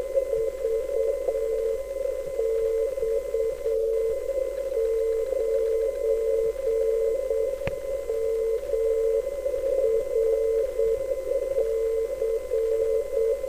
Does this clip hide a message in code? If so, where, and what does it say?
Yes: Morse code from 0.3 to 13.3 s, heard as a keyed tone, about 8 wpm, 450 Hz. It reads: VX9MRC